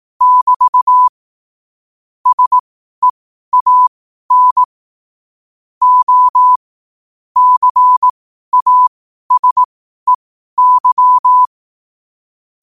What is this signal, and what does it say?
Morse code, heard as a keyed tone, about 18 wpm, spaced as for 12 wpm, 1000 Hz. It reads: <BT> SEAN O CASEY
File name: qotdfiles / 12wpm